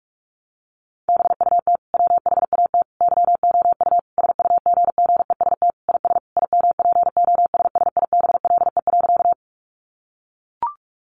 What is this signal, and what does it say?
Morse code, heard as a keyed tone, about 44 wpm, 700 Hz. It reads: BUT WHAT YOU SUGGEST IS IMPOSSIBLE.
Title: Sentences from Top 1300 Words 45wpm